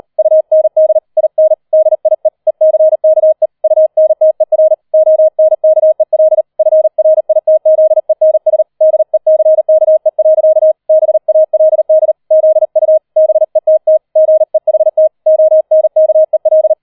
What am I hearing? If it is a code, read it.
Morse code, heard as a keyed tone, about 38 wpm, 630 Hz. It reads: UND IN DIE ECKE UNTER ONKEL FRITZENS DECKE. BALD ZU BETT GEHT ONKEL
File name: 40wpm